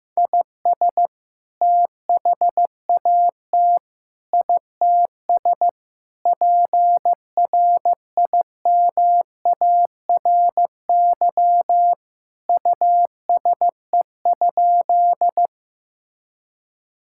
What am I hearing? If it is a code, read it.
Morse code, heard as a keyed tone, about 15 wpm, 700 Hz. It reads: IS THAT ITS PRIMARY USE?
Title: Sentences from Top 1000 Words - Review Part 5 - 15wpm